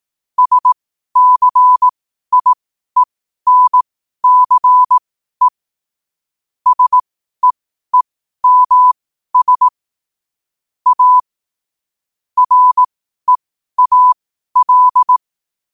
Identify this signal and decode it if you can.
Morse code, heard as a keyed tone, about 18 wpm, spaced as for 12 wpm, 1000 Hz. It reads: SCIENCE SEEMS A REAL
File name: 12wpm